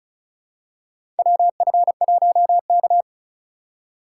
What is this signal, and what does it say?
Morse code, heard as a keyed tone, about 34 wpm, 700 Hz. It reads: WF1K
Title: USA Call Signs - 4 Characters - 4x Character Spacing - 35wpm